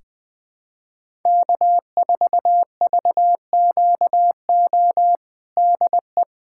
Morse code, heard as a keyed tone, about 20 wpm, 700 Hz. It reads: K4VQO DE
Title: USA Callsign DE Callsign - 5 Characters - 20wpm